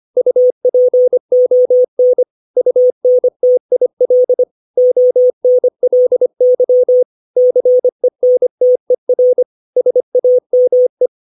Morse code, heard as a keyed tone, about 25 wpm, 500 Hz. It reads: UPON UNTIL ONLY CENTER SAME
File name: 632 words / 632 Words 25WPM